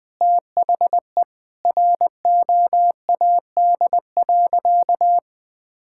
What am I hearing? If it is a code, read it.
Morse code, heard as a keyed tone, about 20 wpm, 700 Hz. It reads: THE ROAD.
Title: Sentences with 4-Word Phrases - 20wpm